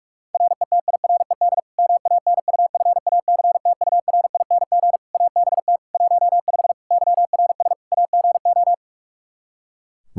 Morse code, heard as a keyed tone, about 41 wpm, 700 Hz. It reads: RETIRED MANUFACTURING ABT 15 YRS AGO
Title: Practice QSO - Exchange 3 - More Info - 45wpm